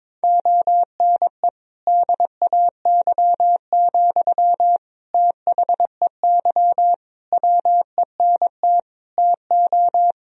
Morse code, heard as a keyed tone, about 21 wpm, 700 Hz. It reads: ONE DAY, THEY WENT TO
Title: Sentences from Top 1000 Words - Review Part 2 - 22wpm